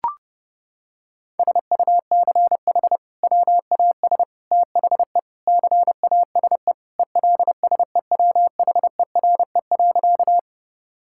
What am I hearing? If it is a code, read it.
Morse code, heard as a keyed tone, about 32 wpm, 700 Hz. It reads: SUCH WAS THE CASE ELSEWHERE.
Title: Sentences from Top 1600 Words 30wpm